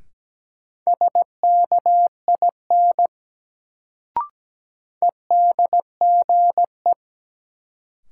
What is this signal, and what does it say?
Morse code, heard as a keyed tone, about 17 wpm, 700 Hz. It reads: SKIN EDGE
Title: Sentences from Top 1300 Words - New Word Drill Alt - 17wpm